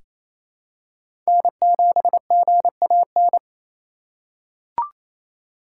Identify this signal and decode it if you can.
Morse code, heard as a keyed tone, about 27 wpm, 700 Hz. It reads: N7GAN
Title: USA Call Signs - 5 Characters - 4x Character Spacing - 28wpm